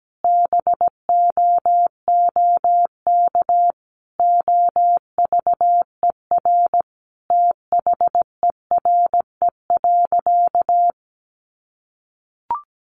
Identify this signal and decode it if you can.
Morse code, heard as a keyed tone, about 17 wpm, 700 Hz. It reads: BOOK OVER THERE.
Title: Sentences from Top 1200 Words 17wpm